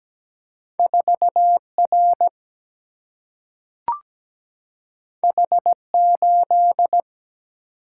Morse code, heard as a keyed tone, about 17 wpm, 700 Hz. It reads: 4R H8
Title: Single-Letter Single-Number 17wpm